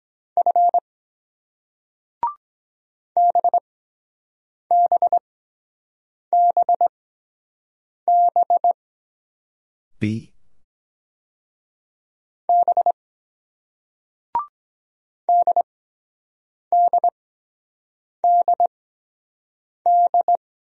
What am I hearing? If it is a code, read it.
Morse code, heard as a keyed tone, about 23 wpm, spaced as for 7 wpm, 700 Hz. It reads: F BBBB B DDDD